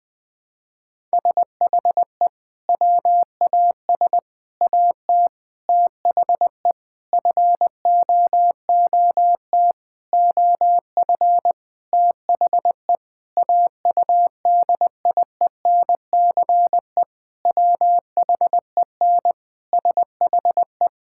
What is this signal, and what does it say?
Morse code, heard as a keyed tone, about 20 wpm, 700 Hz. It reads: SHE WAS AT THE FOOT OF THE AUDIENCE WHEN SHE
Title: Sentences from Top 1000 Words - Review Part 5 - 20wpm